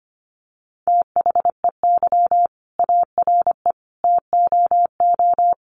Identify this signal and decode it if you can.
Morse code, heard as a keyed tone, about 25 wpm, 700 Hz. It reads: THEY ARE TOO
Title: Sentences from Top 1400 Words 25wpm